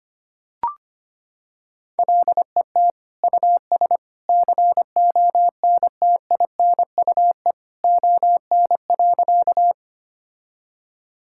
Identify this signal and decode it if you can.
Morse code, heard as a keyed tone, about 25 wpm, 700 Hz. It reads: LET US CONTINUE ON.
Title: Sentences from Top 1000 Words - Review Part 6 - 25wpm